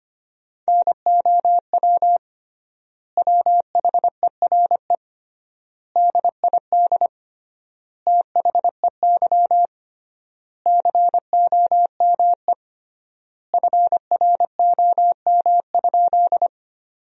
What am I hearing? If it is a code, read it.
Morse code, heard as a keyed tone, about 25 wpm, 700 Hz. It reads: NOW WHERE DID THEY COME FROM?
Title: 3x Word-Spacing - Sentences from Top 200 Words - 25wpm